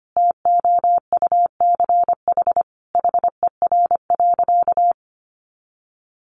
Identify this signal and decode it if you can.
Morse code, heard as a keyed tone, about 25 wpm, 700 Hz. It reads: TOUCH HER.
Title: Sentences from Top 1200 Words 25wpm